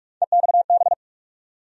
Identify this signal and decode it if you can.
Morse code, heard as a keyed tone, about 42 wpm, 700 Hz. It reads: EXB
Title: USA Call Sign Postfixes - 3 Letters 45wpm